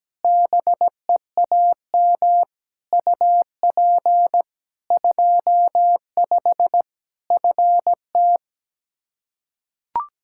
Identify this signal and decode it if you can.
Morse code, heard as a keyed tone, about 17 wpm, 700 Hz. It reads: BEAM UP 25 FT
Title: K1USN SST QSO Practice 17wpm